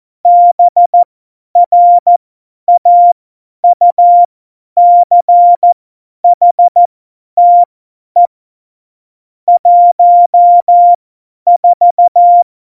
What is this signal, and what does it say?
Morse code, heard as a keyed tone, about 14 wpm, spaced as for 10 wpm, 700 Hz. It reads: BRAUCHTE 14
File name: Morsekurs / Witze / Witz 24. November 2020, 50 F70